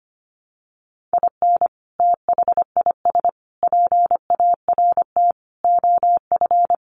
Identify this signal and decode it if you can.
Morse code, heard as a keyed tone, about 25 wpm, 700 Hz. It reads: IN THIS PART OF
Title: Sets of 5 Words 25wpm